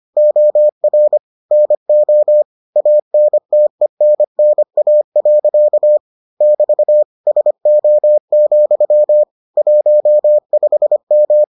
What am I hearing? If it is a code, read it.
Morse code, heard as a keyed tone, about 25 wpm, 600 Hz. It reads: OR NO ANTENNA. <BT> SO, 15M